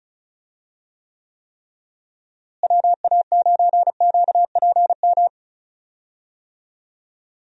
Morse code, heard as a keyed tone, about 34 wpm, 700 Hz. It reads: WA9QPM
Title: USA Call Signs - 6 Characters - Repeated 3x - 35wpm